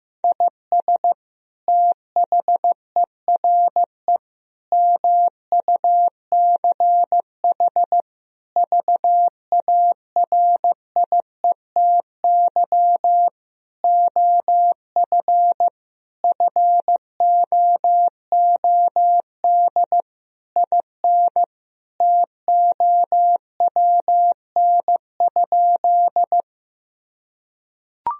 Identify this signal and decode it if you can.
Morse code, heard as a keyed tone, about 15 wpm, 700 Hz. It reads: IS THERE MUCH VARIETY OF FOOD IN TOWN?
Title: Sentences from Top 1000 Words - Review Part 4 - 15wpm